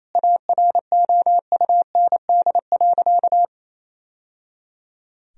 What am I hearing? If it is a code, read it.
Morse code, heard as a keyed tone, about 27 wpm, 700 Hz. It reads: AROUND.
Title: Sentences from Top 1000 Words - Review Part 3 - 28wpm